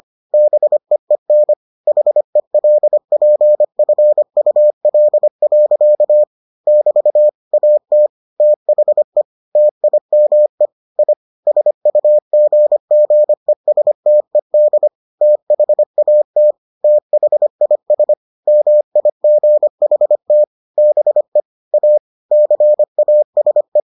Morse code, heard as a keyed tone, about 25 wpm, 600 Hz. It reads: BEEN HELPFUL. <BT> AT THE TIME I SUGGESTED THAT THIS MIGHT BE A CASE